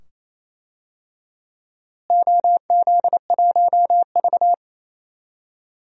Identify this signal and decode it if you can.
Morse code, heard as a keyed tone, about 28 wpm, 700 Hz. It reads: OZ1V
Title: DXCC-100 Call Signs - 4 Characters - Repeated 2x - 28wpm